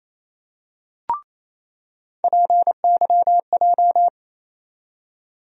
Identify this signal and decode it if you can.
Morse code, heard as a keyed tone, about 28 wpm, 700 Hz. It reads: PYJ
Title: USA Call Sign Postfixes - 3 Letters 28wpm